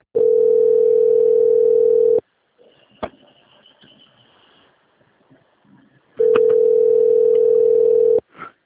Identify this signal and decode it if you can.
Ringback tone (North America), 440 + 480 Hz.
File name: out- - - - - .24375